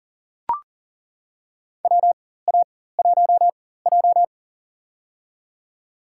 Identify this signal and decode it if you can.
Morse code, heard as a keyed tone, about 40 wpm, spaced as for 19 wpm, 700 Hz. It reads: WA1J
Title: USA Call Signs - 4x Character Spacing - 40wpm